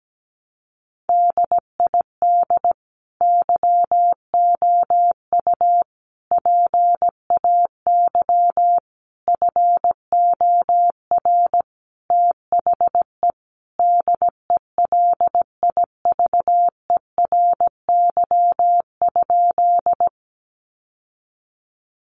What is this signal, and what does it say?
Morse code, heard as a keyed tone, about 17 wpm, 700 Hz. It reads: DID YOU PAY FOR THE DELIVERY?
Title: Sentences from Top 1500 Words 17wpm